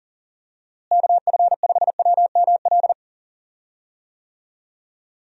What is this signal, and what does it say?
Morse code, heard as a keyed tone, about 40 wpm, 700 Hz. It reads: KF5WML
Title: USA Call Signs - 6 Characters - Repeated 3x - 40wpm